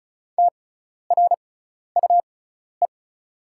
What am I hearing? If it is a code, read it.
Morse code, heard as a keyed tone, about 36 wpm, spaced as for 12 wpm, 700 Hz. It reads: TRUE
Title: 6x Character Spacing - Top 500 Words - 35wpm